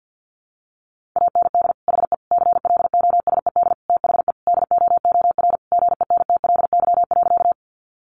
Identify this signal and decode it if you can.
Morse code, heard as a keyed tone, about 50 wpm, 700 Hz. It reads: AND HE CLOSED THE DOOR GENTLY.
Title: Sentences from Top 1500 Words 50wpm